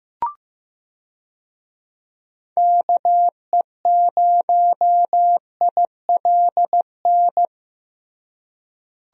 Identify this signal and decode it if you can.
Morse code, heard as a keyed tone, about 16 wpm, 700 Hz. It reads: KE0ILN
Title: USA Call Signs - 6 Characters - Repeated 2x - 15wpm